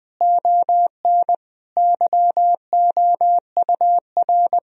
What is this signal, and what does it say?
Morse code, heard as a keyed tone, about 20 wpm, 700 Hz. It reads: ON YOUR
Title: Sentences from Top 1000 Words - Review Part 1 - 20wpm